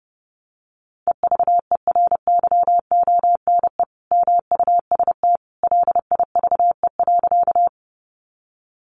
Morse code, heard as a keyed tone, about 30 wpm, 700 Hz. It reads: EVERYONE MUST LIVE.
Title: Sentences from Top 800 Words 30wpm